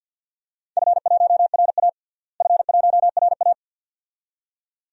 Morse code, heard as a keyed tone, about 49 wpm, 700 Hz. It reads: F1RA F1RA
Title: DXCC-100 Call Signs - 4 Characters - Repeated 2x - 50wpm